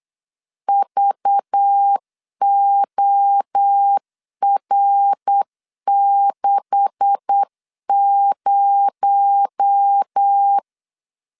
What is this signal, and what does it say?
Morse code, heard as a keyed tone, about 8 wpm, 800 Hz. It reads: VOR60
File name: CW / DET / 1 9 1500C S A 8wpm 5637